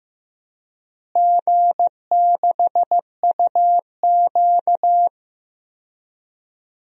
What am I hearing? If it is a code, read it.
Morse code, heard as a keyed tone, about 15 wpm, 700 Hz. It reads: G6UQ
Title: DXCC-100 Call Signs - 4 Characters - Repeated 3x - 15wpm